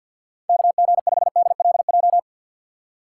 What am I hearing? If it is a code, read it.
Morse code, heard as a keyed tone, about 51 wpm, 700 Hz. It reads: KG5DLJ
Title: USA Call Signs - 6 Characters - 6x Character Spacing - 50wpm